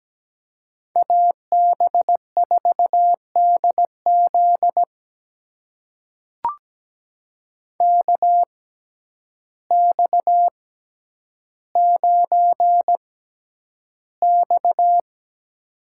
Morse code, heard as a keyed tone, about 17 wpm, spaced as for 9 wpm, 700 Hz. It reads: AB4DZ KX9X
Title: USA Call Signs - 6x Character Spacing - 17wpm